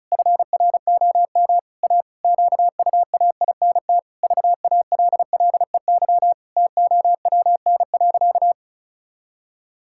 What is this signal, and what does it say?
Morse code, heard as a keyed tone, about 34 wpm, 700 Hz. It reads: FROM A QUAINT VALLEY TOWN.
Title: Pangram Challenge v2 - 35wpm